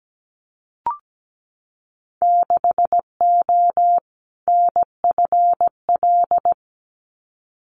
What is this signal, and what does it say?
Morse code, heard as a keyed tone, about 17 wpm, 700 Hz. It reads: 6O NFL
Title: Winter Field Day Exchanges 17wpm